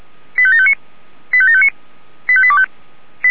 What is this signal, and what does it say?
Five-tone selective call (ZVEI), heard as quick runs of five tones, tones of about 80 ms. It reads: 85759 85759 85825
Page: SDR/ZVEI